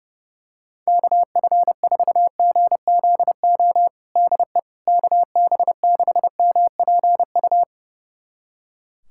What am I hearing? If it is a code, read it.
Morse code, heard as a keyed tone, about 30 wpm, 700 Hz. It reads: KF4GZO DE KB6MPU
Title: USA Callsign DE Callsign - 6 Characters - 30wpm